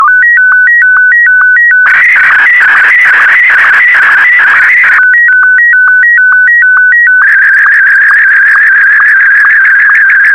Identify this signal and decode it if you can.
SSTV (slow-scan television), Martin 1.